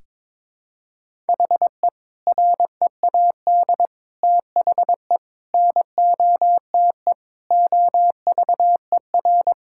Morse code, heard as a keyed tone, about 22 wpm, 700 Hz. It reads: HE READ THE NOTE OVER
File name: practice / 22wpm / Sentences with 4-Word Phrases - 22wpm